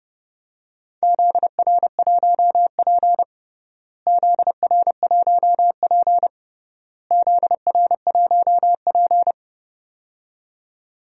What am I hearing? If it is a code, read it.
Morse code, heard as a keyed tone, about 32 wpm, 700 Hz. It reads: ZR1P ZR1P ZR1P
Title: DXCC-100 Call Signs - 4 Characters - Repeated 3x - 30wpm